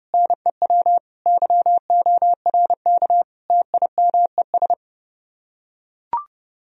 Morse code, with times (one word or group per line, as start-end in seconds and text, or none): 0.1-1.0 NEW
1.3-3.2 YORK
3.5-4.7 TIMES